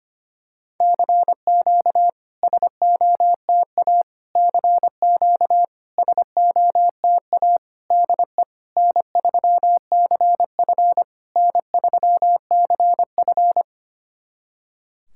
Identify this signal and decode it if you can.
Morse code, heard as a keyed tone, about 25 wpm, 700 Hz. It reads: CQ SOTA CQ SOTA DE N3CF N3CF
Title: SOTA QSO Practice 25wpm